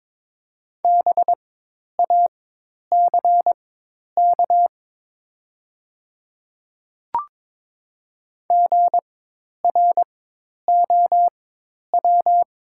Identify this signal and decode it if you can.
Morse code, heard as a keyed tone, about 22 wpm, spaced as for 10 wpm, 700 Hz. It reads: BACK GROW
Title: 4x Character Spacing - Top 300 Words - 22wpm